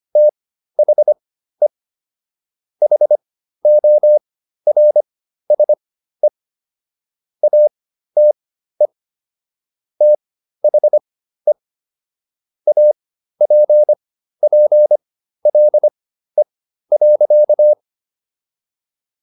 Morse code, heard as a keyed tone, about 25 wpm, spaced as for 13 wpm, 600 Hz. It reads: THE HORSE ATE THE APPLE.